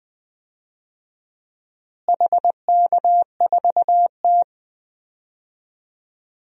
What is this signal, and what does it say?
Morse code, heard as a keyed tone, about 20 wpm, 700 Hz. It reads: HK4T
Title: DXCC-100 Call Signs - 4 Characters - Repeated 2x - 20wpm